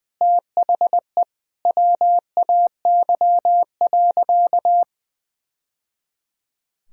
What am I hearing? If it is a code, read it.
Morse code, heard as a keyed tone, about 20 wpm, 700 Hz. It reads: THE WAY.
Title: Sentences from Top 1000 Words - Review Part 4 - 20wpm